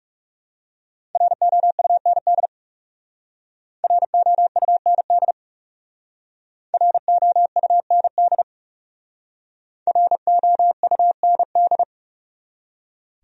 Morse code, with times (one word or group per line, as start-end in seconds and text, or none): 1.1-2.5 ROUND
3.8-5.3 ROUND
6.7-8.4 ROUND
9.9-11.8 ROUND